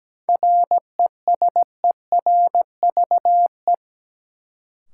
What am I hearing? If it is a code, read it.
Morse code, heard as a keyed tone, about 17 wpm, 700 Hz. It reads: RESERVE
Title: Sentences from Top 1000 Words - Review Part 6 - New Word Drill - 17wpm